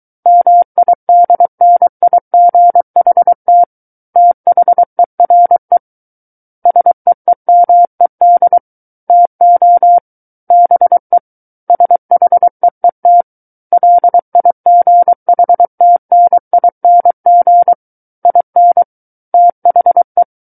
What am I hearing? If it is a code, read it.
Morse code, heard as a keyed tone, about 23 wpm, 700 Hz. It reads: MIDNIGHT THERE SEEMED TO BE SHEET LIGHTNING IN THE
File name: Book One / 14-Chapter 14 - 23 WPM